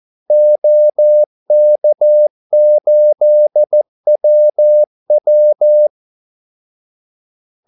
Morse code, heard as a keyed tone, about 14 wpm, 600 Hz. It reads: OK8WW
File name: Callsigns-14wpm